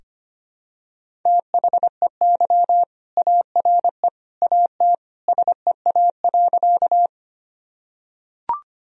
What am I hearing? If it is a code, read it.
Morse code, heard as a keyed tone, about 25 wpm, 700 Hz. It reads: THEY ARE AT SEA.
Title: Sentences from Top 1000 Words - Review Part 1 - 25wpm